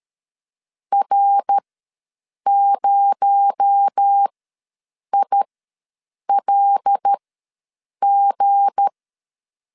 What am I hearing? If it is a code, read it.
Morse code, heard as a keyed tone, about 13 wpm, spaced as for 7 wpm, 800 Hz. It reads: R0ILG
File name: LEZIONI / 7 GF9Z / 7 G ws 12wpm 3140